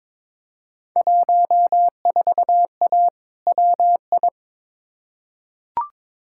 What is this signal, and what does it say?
Morse code, heard as a keyed tone, about 22 wpm, 700 Hz. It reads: 14A WI
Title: Field Day Exchanges Encore - 22wpm